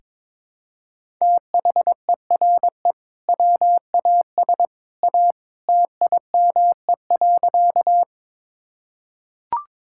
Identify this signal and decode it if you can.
Morse code, heard as a keyed tone, about 22 wpm, 700 Hz. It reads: THERE WAS A TIME.
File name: practice / 22wpm / Sentences with 2-Word Phrases - 22wpm